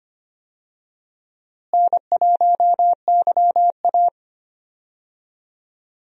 Morse code, heard as a keyed tone, about 25 wpm, 700 Hz. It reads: N1YA
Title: USA Call Signs - 4 Characters - Repeated 3x - 25wpm